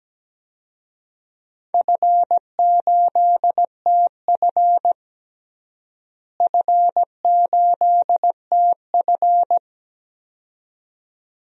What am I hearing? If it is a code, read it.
Morse code, heard as a keyed tone, about 17 wpm, 700 Hz. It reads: F8TF F8TF
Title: DXCC-100 Call Signs - 4 Characters - Repeated 2x - 17wpm